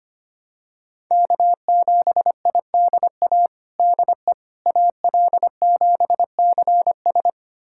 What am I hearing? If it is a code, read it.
Morse code, heard as a keyed tone, about 25 wpm, 700 Hz. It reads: K7IDA DE AL7CS
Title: USA Callsign DE Callsign - 5 Characters - 25wpm